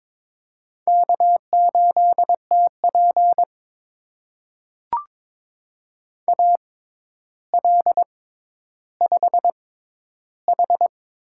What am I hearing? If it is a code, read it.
Morse code, heard as a keyed tone, about 22 wpm, spaced as for 11 wpm, 700 Hz. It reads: K8TP AL5H
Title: USA Call Signs - 6x Character Spacing - 22wpm